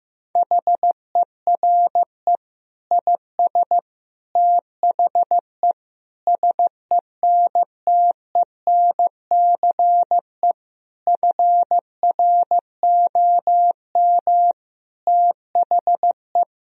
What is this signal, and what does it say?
Morse code, heard as a keyed tone, about 16 wpm, 700 Hz. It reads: HERE IS THE SENTENCE FROM THE
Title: Sentences from Top 1000 Words - Review Part 1 - 15wpm